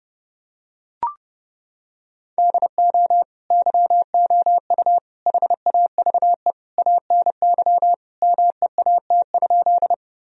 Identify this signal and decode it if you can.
Morse code, heard as a keyed tone, about 30 wpm, 700 Hz. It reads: DO YOU HAVE ANY MEAT?
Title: Sentences from Top 1000 Words - Review Part 3 - 30wpm